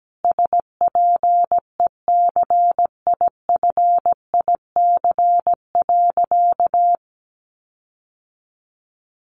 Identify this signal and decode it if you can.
Morse code, heard as a keyed tone, about 17 wpm, 700 Hz. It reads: SPECIFIC.
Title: Sentences from Top 1300 Words 17wpm